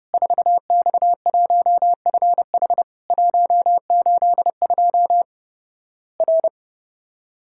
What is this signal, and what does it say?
Morse code, heard as a keyed tone, about 32 wpm, 700 Hz. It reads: 4X1FH 182 R
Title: Worked All Europe DX Contest - QTC Exchange - 30wpm